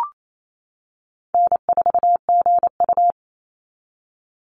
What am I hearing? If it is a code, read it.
Morse code, heard as a keyed tone, about 28 wpm, 700 Hz. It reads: N4GU
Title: Easy Call Signs - 28wpm